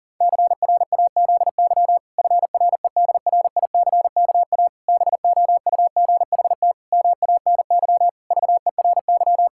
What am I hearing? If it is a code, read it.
Morse code, heard as a keyed tone, about 40 wpm, 700 Hz. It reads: CRAZY FREDRICKA BOUGHT MANY VERY
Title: Pangram Challenge v2 - 40wpm